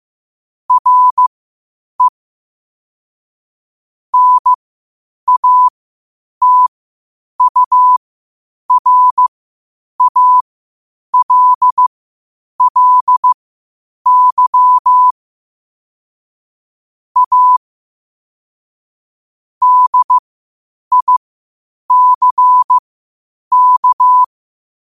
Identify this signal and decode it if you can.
Morse code, heard as a keyed tone, about 15 wpm, spaced as for 8 wpm, 1000 Hz. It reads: RE NATURALLY A DICK
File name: qotdfiles / 8wpm